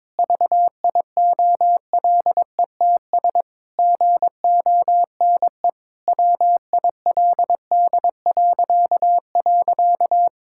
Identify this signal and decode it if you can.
Morse code, heard as a keyed tone, about 22 wpm, 700 Hz. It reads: VIOLETS GONE WILD..
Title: Pangram Challenge v2 - 22wpm